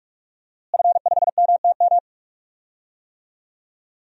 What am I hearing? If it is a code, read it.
Morse code, heard as a keyed tone, about 41 wpm, 700 Hz. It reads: F5MTM
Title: DXCC-100 Call Signs - 5 Characters - Repeated 3x - 45wpm